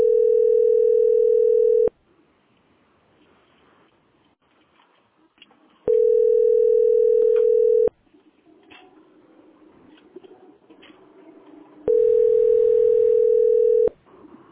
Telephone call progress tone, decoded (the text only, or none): ringback tone